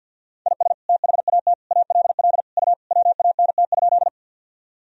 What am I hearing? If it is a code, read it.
Morse code, heard as a keyed tone, about 51 wpm, 700 Hz. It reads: IS THAT ALL U WANT?
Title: Sentences from Top 200 Words and Abbreviations 50wpm